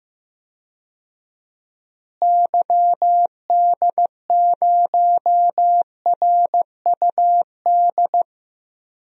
Morse code, heard as a keyed tone, about 15 wpm, 700 Hz. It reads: YD0RUD
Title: DXCC-100 Call Signs - 6 Characters - Repeated 2x - 15wpm